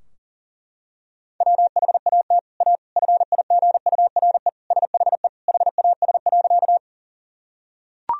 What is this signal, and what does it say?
Morse code, heard as a keyed tone, about 40 wpm, 700 Hz. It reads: WHAT A FIGURE SHE HAS.
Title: Sentences from Top 1000 Words - Review Part 2 - 40wpm